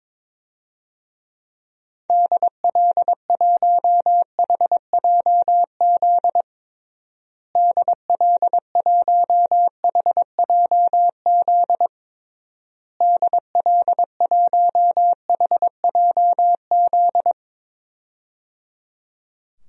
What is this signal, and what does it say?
Morse code, heard as a keyed tone, about 22 wpm, 700 Hz. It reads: DL1HJZ DL1HJZ DL1HJZ
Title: DXCC-100 Call Signs - 6 Characters - Repeated 3x - 22wpm